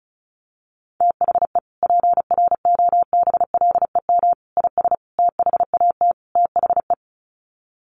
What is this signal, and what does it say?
Morse code, heard as a keyed tone, about 34 wpm, 700 Hz. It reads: THE PROBLEM IS THAT THE
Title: Sets of 5 Words Encore2 35wpm